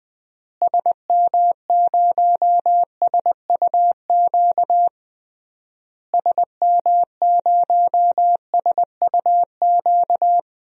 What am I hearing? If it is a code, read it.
Morse code, heard as a keyed tone, about 20 wpm, 700 Hz. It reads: SM0SUQ SM0SUQ
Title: DXCC-100 Call Signs - 6 Characters - Repeated 3x - 20wpm